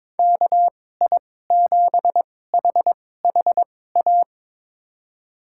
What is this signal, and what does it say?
Morse code, heard as a keyed tone, about 22 wpm, spaced as for 16 wpm, 700 Hz. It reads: KI7HHA
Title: USA Call Signs - 6 Characters - 2x Character Spacing - 22wpm